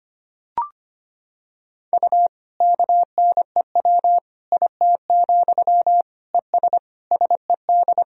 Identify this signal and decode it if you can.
Morse code, heard as a keyed tone, about 25 wpm, 700 Hz. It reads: U KNEW IT, ES SED
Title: Sentences from Top 500 Words and Abbreviations 25wpm